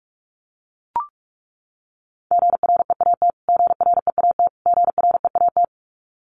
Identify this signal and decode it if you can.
Morse code, heard as a keyed tone, about 44 wpm, 700 Hz. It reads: GREAT GREAT GREAT
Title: Top 200 Words Repeated 45wpm